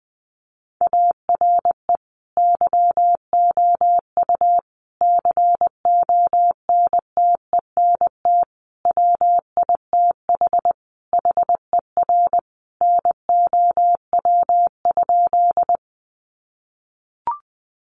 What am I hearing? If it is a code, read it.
Morse code, heard as a keyed tone, about 20 wpm, 700 Hz. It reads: ARE YOU CONTENT WITH HER NOW?
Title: Sentences from Top 900 Words 20wpm